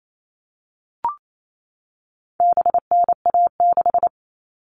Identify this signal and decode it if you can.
Morse code, heard as a keyed tone, about 28 wpm, 700 Hz. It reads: BNA6
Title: All Cut Numbers - 4-Digit 28wpm